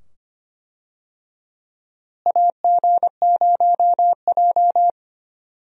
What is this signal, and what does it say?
Morse code, heard as a keyed tone, about 25 wpm, 700 Hz. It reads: AG0J
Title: USA Call Signs - 4 Characters - Repeated 2x - 25wpm